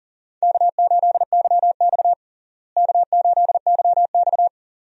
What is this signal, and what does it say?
Morse code, heard as a keyed tone, about 40 wpm, 700 Hz. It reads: K8YX K8YX
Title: USA Call Signs - 4 Characters - Repeated 2x - 40wpm